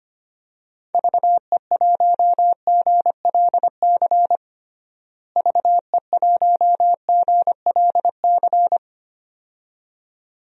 Morse code, heard as a keyed tone, about 25 wpm, 700 Hz. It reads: VE1GLC VE1GLC
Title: DXCC-100 Call Signs - 6 Characters - Repeated 2x - 25wpm